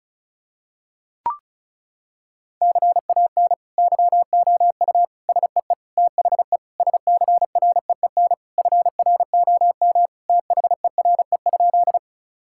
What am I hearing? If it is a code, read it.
Morse code, heard as a keyed tone, about 34 wpm, 700 Hz. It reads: CAN YOU SEE THE SCREEN FROM THERE?
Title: Sentences from Top 1000 Words - Review Part 6 - 35wpm